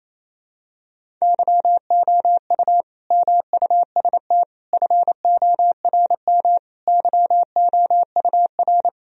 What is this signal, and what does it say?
Morse code, heard as a keyed tone, about 28 wpm, 700 Hz. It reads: YOU MUST FORM YOUR
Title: Sentences from Top 1000 Words - Review Part 1 - 28wpm